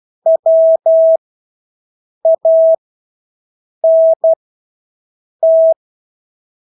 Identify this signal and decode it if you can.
Morse code, heard as a keyed tone, about 12 wpm, spaced as for 6 wpm, 650 Hz. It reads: WANT
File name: Engelske ord 12-6 wpm